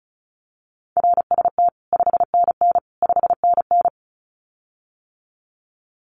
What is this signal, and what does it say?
Morse code, heard as a keyed tone, about 36 wpm, 700 Hz. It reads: RST 5NN 5NN
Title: Common QSO Phrases Encore 35wpm